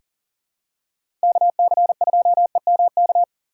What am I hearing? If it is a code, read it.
Morse code, heard as a keyed tone, about 38 wpm, 700 Hz. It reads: KC2EMK
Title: USA Call Signs - 6 Characters - 3x Character Spacing - 40wpm